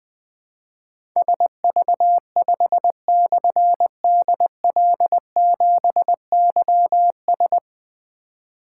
Morse code, heard as a keyed tone, about 21 wpm, 700 Hz. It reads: SV5/DL7YS
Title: Difficult Call Signs 20wpm